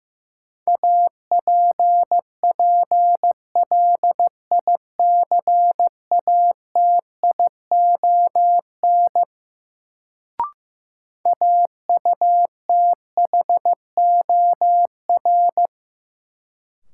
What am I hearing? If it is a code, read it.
Morse code, heard as a keyed tone, about 15 wpm, 700 Hz. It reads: APPLICATION AUTHOR
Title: Sentences from Top 1200 Words - New Word Drill Alt - 15wpm